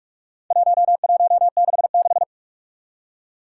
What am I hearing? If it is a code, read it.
Morse code, heard as a keyed tone, about 46 wpm, 700 Hz. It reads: 1166
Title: Four Challenging Letter-Number Combinations 45wpm